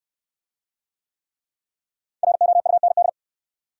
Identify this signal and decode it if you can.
Morse code, heard as a keyed tone, about 69 wpm, 700 Hz. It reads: ROUND